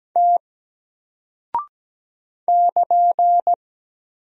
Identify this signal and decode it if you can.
Morse code, heard as a keyed tone, about 17 wpm, 700 Hz. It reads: T <KN>
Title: Single Character (Letter:Number:Punctuation:Prosign) 17wpm